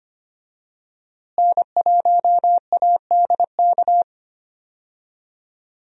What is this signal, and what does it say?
Morse code, heard as a keyed tone, about 25 wpm, 700 Hz. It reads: N1ADK
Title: USA Call Signs - 5 Characters - Repeated 3x - 25wpm